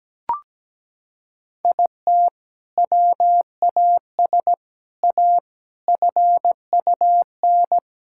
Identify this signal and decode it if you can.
Morse code, heard as a keyed tone, about 17 wpm, 700 Hz. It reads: IT WAS A FUN
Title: Sentences from Top 1000 Words - Review Part 5 - 17wpm